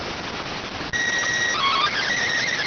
SSTV (slow-scan television), PD 290.